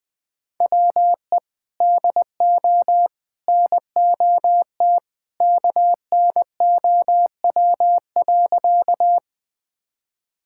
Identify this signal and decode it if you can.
Morse code, heard as a keyed tone, about 20 wpm, 700 Hz. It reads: WE DO NOT KNOW.
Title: Sentences with 3-Word Phrases - 20wpm